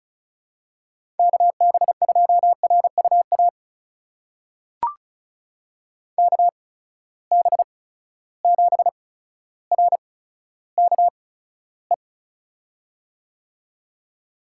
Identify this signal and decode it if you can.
Morse code, heard as a keyed tone, about 36 wpm, spaced as for 15 wpm, 700 Hz. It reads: KB2RUA KB7RKE
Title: USA Call Signs - 6 Characters - 8x Character Spacing - 35wpm